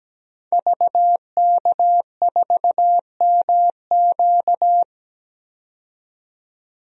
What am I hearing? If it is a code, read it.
Morse code, heard as a keyed tone, about 17 wpm, 700 Hz. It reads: VK4MQ
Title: DXCC-100 Call Signs - 5 Characters - 17wpm